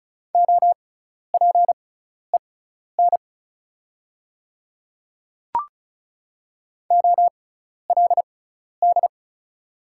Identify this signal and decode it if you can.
Morse code, heard as a keyed tone, about 36 wpm, spaced as for 12 wpm, 690 Hz. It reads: OPEN OLD
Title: 6x Character Spacing - Top 500 Words - 35wpm